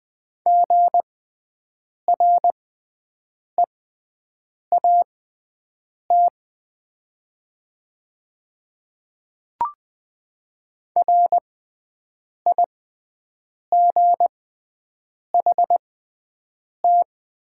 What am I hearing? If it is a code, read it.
Morse code, heard as a keyed tone, about 20 wpm, spaced as for 7 wpm, 700 Hz. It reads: GREAT RIGHT
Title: 6x Character Spacing - Top 200 Words - 20wpm